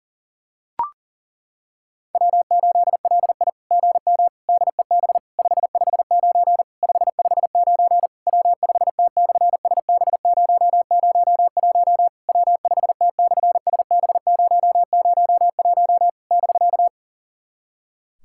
Morse code, heard as a keyed tone, about 40 wpm, 700 Hz. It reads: W8LI GM DEB 559 559 W5T/SB001 W5T/SB001 <BK>
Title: SOTA QSO Practice 40wpm